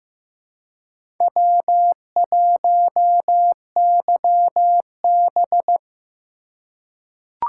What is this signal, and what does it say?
Morse code, heard as a keyed tone, about 15 wpm, 700 Hz. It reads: W1YB
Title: USA Call Signs - 4 Characters - 2x Character Spacing - 15wpm